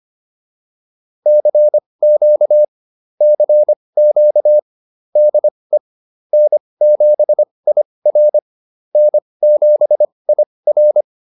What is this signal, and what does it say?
Morse code, heard as a keyed tone, about 25 wpm, spaced as for 20 wpm, 600 Hz. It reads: CQ CQ DE N7IR N7IR